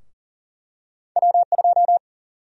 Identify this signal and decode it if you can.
Morse code, heard as a keyed tone, about 40 wpm, 700 Hz. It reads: W2
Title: USA Call Sign Prefixes - Repeated 2x - 40wpm